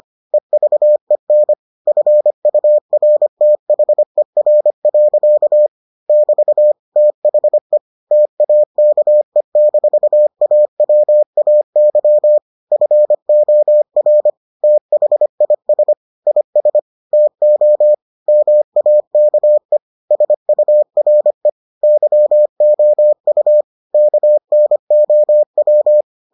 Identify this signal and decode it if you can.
Morse code, heard as a keyed tone, about 25 wpm, 600 Hz. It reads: EVEN FURTHER. <BT> THE TAKE-AWAY FOR THIS IS TO MAKE SURE YOU KNOW